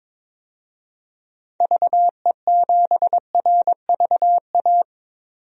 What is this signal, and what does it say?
Morse code, heard as a keyed tone, about 22 wpm, 700 Hz. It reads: VE7RVA
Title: DXCC-100 Call Signs - 6 Characters - 22wpm